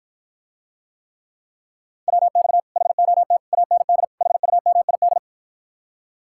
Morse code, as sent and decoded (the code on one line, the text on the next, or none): .-- -..- / .... --- - / .- -. -.. / .... ..- -- .. -..
WX HOT AND HUMID